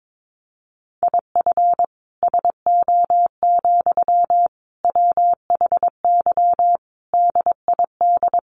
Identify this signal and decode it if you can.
Morse code, heard as a keyed tone, about 22 wpm, 700 Hz. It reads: IF SO, WHY DID
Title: Sentences from Top 1400 Words 22wpm